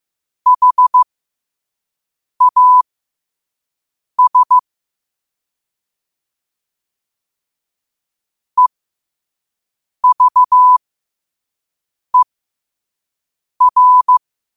Morse code, heard as a keyed tone, about 15 wpm, spaced as for 5 wpm, 1000 Hz. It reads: HAS EVER
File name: qotdfiles / 5wpm